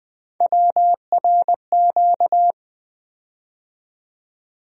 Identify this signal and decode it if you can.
Morse code, heard as a keyed tone, about 20 wpm, 700 Hz. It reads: WRQ
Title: USA Call Sign Postfixes - 3 Letters 20wpm